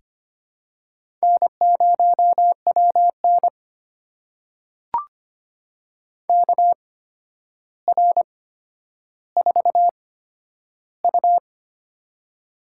Morse code, heard as a keyed tone, about 25 wpm, spaced as for 11 wpm, 700 Hz. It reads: N0WN KR4U
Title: USA Call Signs - 4 Characters - 8x Character Spacing - 25wpm